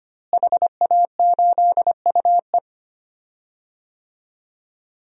Morse code, heard as a keyed tone, about 25 wpm, 700 Hz. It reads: HA8UE
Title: DXCC-100 Call Signs - 5 Characters - Repeated 2x - 25wpm